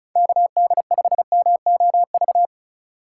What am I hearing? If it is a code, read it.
Morse code, heard as a keyed tone, about 34 wpm, 690 Hz. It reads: KD5MOV
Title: USA Call Signs - 4x Character Spacing - 35wpm